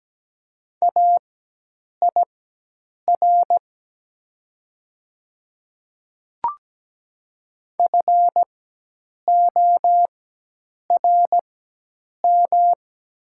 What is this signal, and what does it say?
Morse code, heard as a keyed tone, about 17 wpm, spaced as for 8 wpm, 700 Hz. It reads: AIR FORM
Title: 4x Character Spacing - Top 500 Words - 17wpm